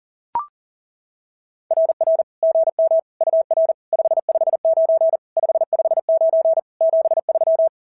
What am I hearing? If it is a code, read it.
Morse code, heard as a keyed tone, about 40 wpm, 690 Hz. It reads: RR GM UR 559 559 73
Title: SOTA QSO Practice 40wpm